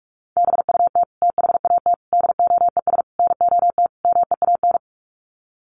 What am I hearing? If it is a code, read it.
Morse code, heard as a keyed tone, about 44 wpm, 700 Hz. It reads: BUT THAT DOES NOT MEAN
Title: Sets of 5 Words 45wpm